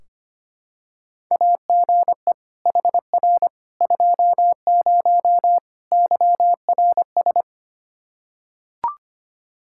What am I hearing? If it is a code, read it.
Morse code, heard as a keyed tone, about 25 wpm, 700 Hz. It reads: AGE HR 20 YRS
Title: Practice QSO - Exchange 3 - More Info - 25wpm